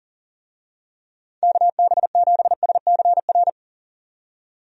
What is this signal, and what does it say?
Morse code, heard as a keyed tone, about 40 wpm, 700 Hz. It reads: KB7SCR
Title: USA Call Signs - 6 Characters - 40wpm